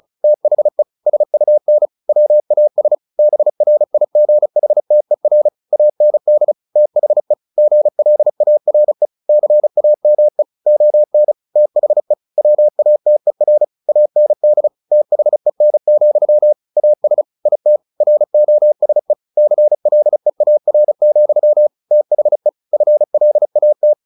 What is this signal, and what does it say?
Morse code, heard as a keyed tone, about 34 wpm, 600 Hz. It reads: THE SUN WAS BRIGHTER AND THE GLARE CAME ON THE WATER AND THEN, AS IT ROSE CLEAR, THE FLAT
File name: the old man and the sea / the old man and the sea 1 35wpm